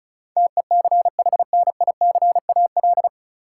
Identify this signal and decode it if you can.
Morse code, heard as a keyed tone, about 34 wpm, 700 Hz. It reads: TECHNICAL
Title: Sentences from Top 1100 Words - New Word Drill - 35wpm